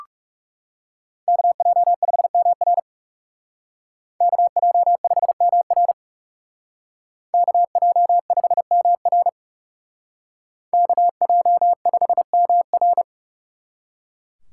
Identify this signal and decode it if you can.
Morse code, heard as a keyed tone, about 36 wpm, 700 Hz. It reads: KJ5MR KJ5MR KJ5MR KJ5MR
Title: Speed Racing - USA 2x2 Call Signs 30wpm